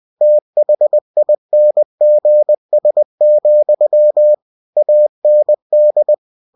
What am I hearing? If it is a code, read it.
Morse code, heard as a keyed tone, about 20 wpm, 600 Hz. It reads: THINGS, AND